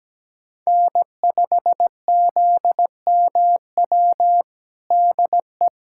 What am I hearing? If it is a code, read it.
Morse code, heard as a keyed tone, about 17 wpm, 700 Hz. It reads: N5ZMW DE
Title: USA Callsign DE Callsign - 5 Characters - 17wpm